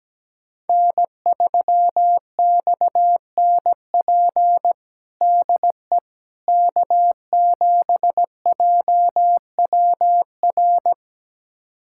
Morse code, heard as a keyed tone, about 17 wpm, 700 Hz. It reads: N3XNP DE K7JWR
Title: USA Callsign DE Callsign - 5 Characters - 17wpm